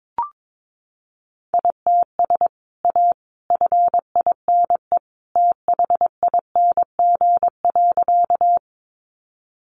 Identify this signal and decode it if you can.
Morse code, heard as a keyed tone, about 22 wpm, 700 Hz. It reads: ITS A FINE THING.
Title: Sentences from Top 600 Words 22wpm